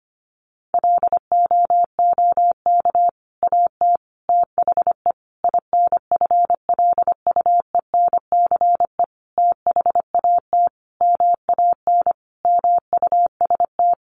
Morse code, heard as a keyed tone, about 25 wpm, 700 Hz. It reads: LOOK AT THE INFLUENCE THAT MAN MUST
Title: Sentences from Top 1100 Words 25wpm